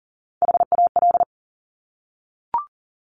Morse code, heard as a keyed tone, about 41 wpm, 690 Hz. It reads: HAL
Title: Top Three Letter Combinations in Words 40wpm